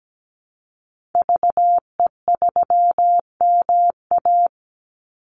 Morse code, heard as a keyed tone, about 17 wpm, 700 Hz. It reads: VE3MA
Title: Call Signs - 17wpm